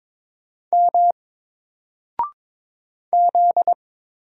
Morse code, heard as a keyed tone, about 22 wpm, 700 Hz. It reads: M Z
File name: Single-Letter-Number-22wpm